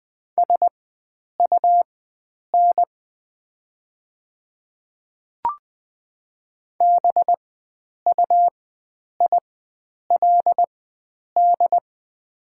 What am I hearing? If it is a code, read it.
Morse code, heard as a keyed tone, about 20 wpm, spaced as for 9 wpm, 700 Hz. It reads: SUN BUILD
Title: 4x Character Spacing - Top 300 Words - 20wpm